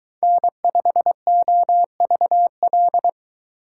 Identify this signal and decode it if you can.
Morse code, heard as a keyed tone, about 21 wpm, 700 Hz. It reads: N5OVL